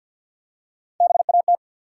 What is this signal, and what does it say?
Morse code, heard as a keyed tone, about 48 wpm, 700 Hz. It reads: BAT
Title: Three Letter Words - 50wpm